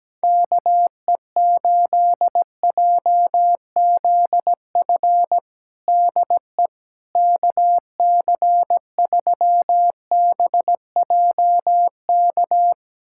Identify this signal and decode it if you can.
Morse code, heard as a keyed tone, about 17 wpm, 700 Hz. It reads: KE8JZF DE KC3BJK
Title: USA Callsign DE Callsign - 6 Characters - 17wpm